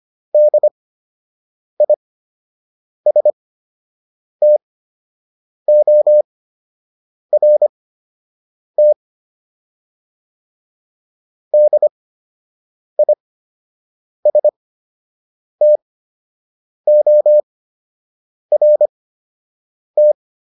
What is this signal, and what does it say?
Morse code, heard as a keyed tone, about 25 wpm, spaced as for 7 wpm, 600 Hz. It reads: DISTORT DISTORT
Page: fundamental-practice-files